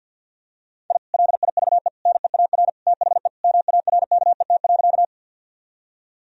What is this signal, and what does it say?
Morse code, heard as a keyed tone, about 50 wpm, 700 Hz. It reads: I LIVE NEAR THE MARKET.
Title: Sentences from Top 1000 Words - Review Part 3 - 50wpm